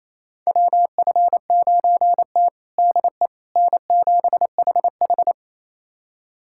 Morse code, heard as a keyed tone, about 28 wpm, 700 Hz. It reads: WF9T DE N7HH
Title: USA Callsign DE Callsign - 4 Characters - 28wpm